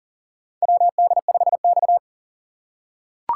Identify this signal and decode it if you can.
Morse code, heard as a keyed tone, about 40 wpm, 700 Hz. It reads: WD5X
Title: USA Call Signs - 4 Characters - 3x Character Spacing - 40wpm